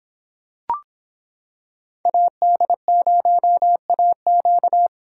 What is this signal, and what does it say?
Morse code, heard as a keyed tone, about 26 wpm, 700 Hz. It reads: AD0AQ